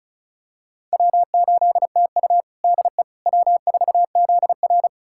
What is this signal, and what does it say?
Morse code, heard as a keyed tone, about 36 wpm, 700 Hz. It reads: W8TU DE W4ZR
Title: USA Callsign DE Callsign - 4 Characters - 35wpm